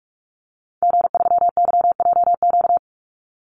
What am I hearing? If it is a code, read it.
Morse code, heard as a keyed tone, about 44 wpm, 700 Hz. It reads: G3YJQ
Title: Call Signs Repeated 2x - 45wpm